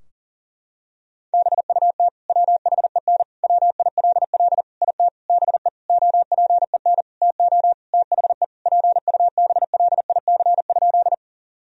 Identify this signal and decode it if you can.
Morse code, heard as a keyed tone, about 40 wpm, 700 Hz. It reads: BUT WHEN WILL IT BE OPEN TO THE PUBLIC?
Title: Sentences from Top 1000 Words - Review Part 3 - 40wpm